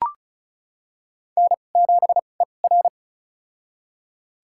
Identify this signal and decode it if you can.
Morse code, heard as a keyed tone, about 34 wpm, spaced as for 25 wpm, 700 Hz. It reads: N7ER
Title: USA Call Signs - 4 Characters - 2x Character Spacing - 35wpm